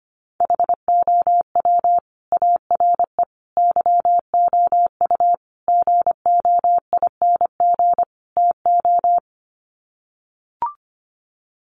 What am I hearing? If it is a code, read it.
Morse code, heard as a keyed tone, about 25 wpm, 700 Hz. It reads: HOW ARE YOU GOING TO
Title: Sets of 5 Words 25wpm